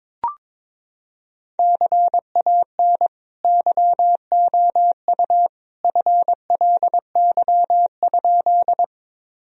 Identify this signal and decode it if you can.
Morse code, heard as a keyed tone, about 21 wpm, 700 Hz. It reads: CAN YOU FLY?
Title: Sentences from Top 1000 Words - Review Part 2 - 22wpm